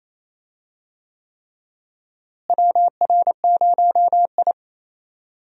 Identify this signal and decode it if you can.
Morse code, heard as a keyed tone, about 27 wpm, 700 Hz. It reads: WR0I
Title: USA Call Signs - 4 Characters - Repeated 3x - 28wpm